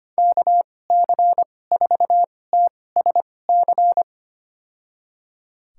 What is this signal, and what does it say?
Morse code, heard as a keyed tone, about 25 wpm, spaced as for 18 wpm, 700 Hz. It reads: KC4TSC
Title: USA Call Signs - 6 Characters - 2x Character Spacing - 25wpm